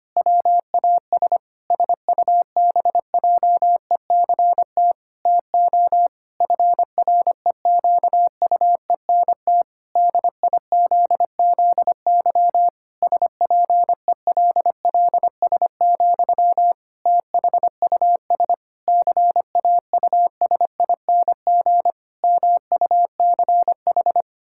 Morse code, heard as a keyed tone, about 25 wpm, 700 Hz. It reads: WAS SUBJECT TO FREQUENT DIZZY SPELLS, THUS CAUSING MUCH
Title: Pangram Challenge v2 - 25wpm